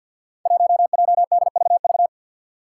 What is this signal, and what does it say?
Morse code, heard as a keyed tone, about 48 wpm, 700 Hz. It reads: 1JDVV
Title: Five Challenging Letter-Number Combinations 50wpm